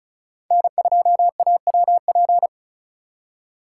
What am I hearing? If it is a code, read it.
Morse code, heard as a keyed tone, about 36 wpm, 700 Hz. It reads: N2AWP
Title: USA Call Signs - 5 Characters - 3x Character Spacing - 35wpm